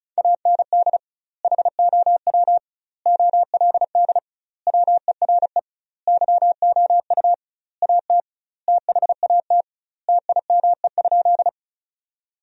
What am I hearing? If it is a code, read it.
Morse code, heard as a keyed tone, about 36 wpm, 700 Hz. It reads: AND HOW OLD WERE YOU AT THAT TIME?
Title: 2x Word-Spacing - Sentences from Top 200 Words - 35wpm